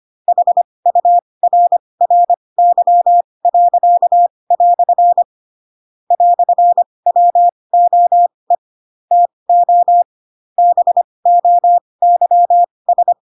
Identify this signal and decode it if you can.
Morse code, heard as a keyed tone, about 25 wpm, spaced as for 20 wpm, 700 Hz. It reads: HURRY." "WOE TO BOYS
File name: Pinocchio / Pinocchio ch4 15 25 ik1hge